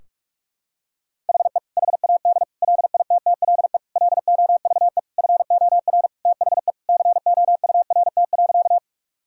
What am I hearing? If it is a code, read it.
Morse code, heard as a keyed tone, about 46 wpm, 700 Hz. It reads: HE HAD LITTLE LOVE FOR THE COURT.
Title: Sentences from Top 1600 Words 45wpm